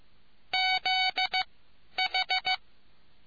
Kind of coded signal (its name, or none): Morse code